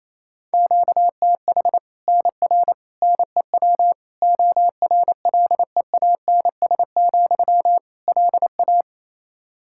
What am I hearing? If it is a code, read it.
Morse code, heard as a keyed tone, about 28 wpm, 700 Hz. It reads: QTH NR NEW ORLEANS, LA
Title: K1USN SST QSO Practice 28wpm